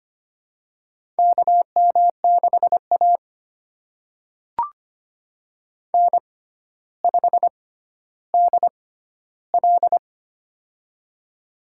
Morse code, heard as a keyed tone, about 25 wpm, spaced as for 13 wpm, 700 Hz. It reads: KM6A N5DL
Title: USA Call Signs - 4 Characters - 6x Character Spacing - 25wpm